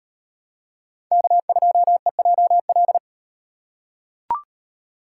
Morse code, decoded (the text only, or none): K2EJL